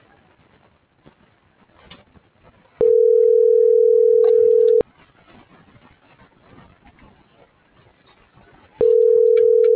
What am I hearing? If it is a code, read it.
Ringback tone (North America), 440 + 480 Hz.